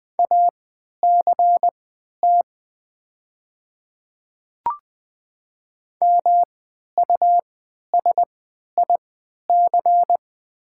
Morse code, heard as a keyed tone, about 20 wpm, spaced as for 11 wpm, 700 Hz. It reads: ACT MUSIC
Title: 3x Character Spacing - Top 300 Words - 20wpm